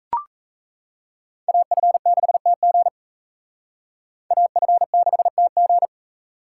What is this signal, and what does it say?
Morse code, heard as a keyed tone, about 35 wpm, 700 Hz. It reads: AF6TG AF6TG